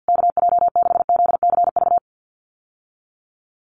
Morse code, heard as a keyed tone, about 49 wpm, 700 Hz. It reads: KJ6ZCV
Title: USA Call Signs Repeated 3x 50wpm